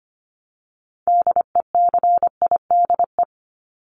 Morse code, heard as a keyed tone, about 25 wpm, 700 Hz. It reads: DECIDE
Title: ED Suffix - Top 100 Words with Base Words 25wpm